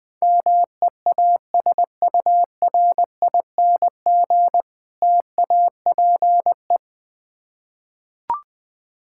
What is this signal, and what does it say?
Morse code, heard as a keyed tone, about 21 wpm, 700 Hz. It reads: MEASURING TAPE
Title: Difficult Words with Hints - 20wpm